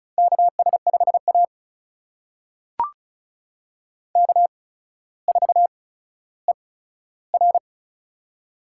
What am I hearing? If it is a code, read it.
Morse code, heard as a keyed tone, about 36 wpm, spaced as for 15 wpm, 700 Hz. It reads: KS5A K4ER
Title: USA Call Signs - 4 Characters - 8x Character Spacing - 35wpm